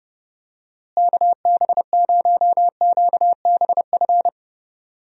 Morse code, heard as a keyed tone, about 30 wpm, 700 Hz. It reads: KB0QBF
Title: USA Call Signs - 6 Characters - Repeated 2x - 30wpm